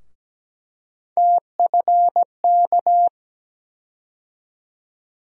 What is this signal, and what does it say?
Morse code, heard as a keyed tone, about 17 wpm, 700 Hz. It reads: TFK
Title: USA Call Sign Postfixes - 3 Letters 17wpm